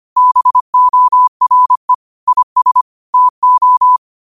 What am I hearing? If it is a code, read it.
Morse code, heard as a keyed tone, about 25 wpm, 1000 Hz. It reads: BORE IS TO
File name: qotdfiles / 25wpm